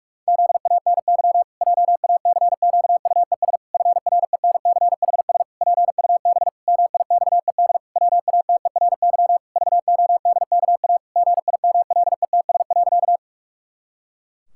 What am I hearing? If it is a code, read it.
Morse code, heard as a keyed tone, about 46 wpm, 700 Hz. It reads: ZANY JACQUES FRENCHS PUB MIXED WATERY VODKA GIMLETS.
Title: Pangram Challenge v2 - 45wpm